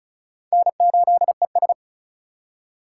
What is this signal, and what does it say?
Morse code, heard as a keyed tone, about 36 wpm, 700 Hz. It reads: N8ES
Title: USA Call Signs - 4 Characters - 4x Character Spacing - 35wpm